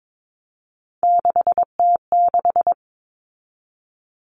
Morse code, heard as a keyed tone, about 22 wpm, 700 Hz. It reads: 6T6
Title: All Cut Numbers - 3-Digit 22wpm